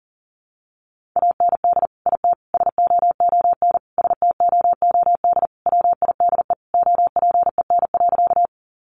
Morse code, heard as a keyed tone, about 37 wpm, 700 Hz. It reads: AND IT SOON STOOD WIDE OPEN.
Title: Sentences from Top 900 Words 40wpm